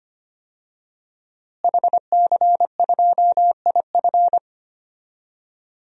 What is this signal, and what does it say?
Morse code, heard as a keyed tone, about 25 wpm, 700 Hz. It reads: HC2IF
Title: DXCC-100 Call Signs - 5 Characters - Repeated 2x - 25wpm